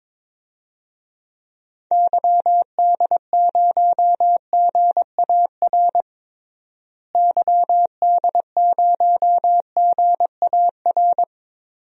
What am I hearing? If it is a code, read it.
Morse code, heard as a keyed tone, about 22 wpm, 700 Hz. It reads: YD0GAR YD0GAR
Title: DXCC-100 Call Signs - 6 Characters - Repeated 2x - 22wpm